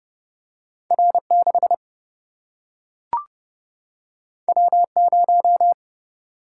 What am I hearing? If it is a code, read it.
Morse code, heard as a keyed tone, about 30 wpm, 700 Hz. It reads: R 6 W 0
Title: Single-Letter Single-Number 30wpm